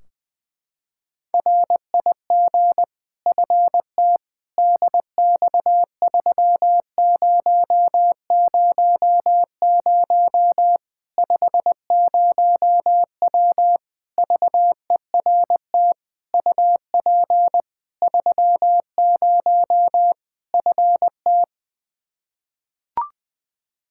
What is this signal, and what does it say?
Morse code, heard as a keyed tone, about 20 wpm, 700 Hz. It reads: RIG FT DX3000 50W VERT UP 30 FT
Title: Practice QSO - Exchange 2 - Info - 20wpm